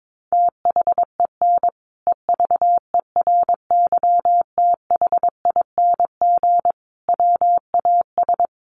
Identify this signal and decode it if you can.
Morse code, heard as a keyed tone, about 22 wpm, 700 Hz. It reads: THEN EVERYTHING WAS
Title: Sentences from Top 700 Words 22wpm